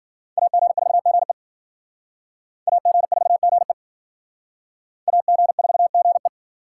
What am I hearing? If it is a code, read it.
Morse code, heard as a keyed tone, about 56 wpm, 700 Hz. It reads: AG4GE AG4GE AG4GE